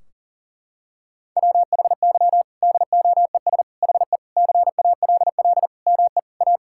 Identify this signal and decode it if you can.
Morse code, heard as a keyed tone, about 40 wpm, 700 Hz. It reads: WHY DOES HE CALL ME A
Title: Sentences from Top 1600 Words 40wpm